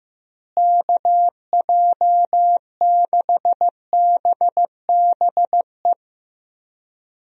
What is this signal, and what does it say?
Morse code, heard as a keyed tone, about 15 wpm, 700 Hz. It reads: KJ6BBE
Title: USA Call Signs - 6 Characters - Repeated 3x - 15wpm